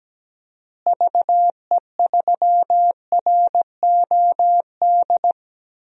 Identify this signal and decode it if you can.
Morse code, heard as a keyed tone, about 17 wpm, 700 Hz. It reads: VE3ROD
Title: DXCC-100 Call Signs - 6 Characters - Repeated 3x - 17wpm